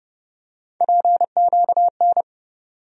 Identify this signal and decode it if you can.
Morse code, heard as a keyed tone, about 30 wpm, 700 Hz. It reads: PQN
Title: USA Call Sign Postfixes - 3 Letters 30wpm